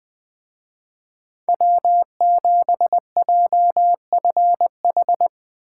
Morse code, heard as a keyed tone, about 22 wpm, 700 Hz. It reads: W7JFH
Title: Speed Racing - USA 1x3 Call Signs 20wpm